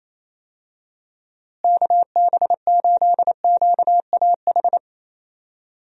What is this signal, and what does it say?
Morse code, heard as a keyed tone, about 28 wpm, 700 Hz. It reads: KB8QAH
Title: USA Call Signs - 6 Characters - Repeated 2x - 28wpm